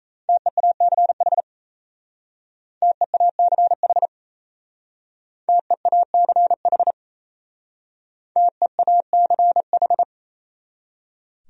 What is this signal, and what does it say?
Morse code, heard as a keyed tone, about 37 wpm, 700 Hz. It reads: TEACH TEACH TEACH TEACH